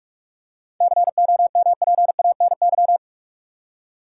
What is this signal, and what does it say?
Morse code, heard as a keyed tone, about 42 wpm, 700 Hz. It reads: COMPANY
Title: Seven Letter Words - 45wpm